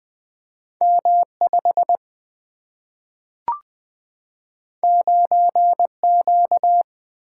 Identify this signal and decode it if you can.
Morse code, heard as a keyed tone, about 20 wpm, 700 Hz. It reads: M 5 9 Q
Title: Single-Letter Single-Number 20wpm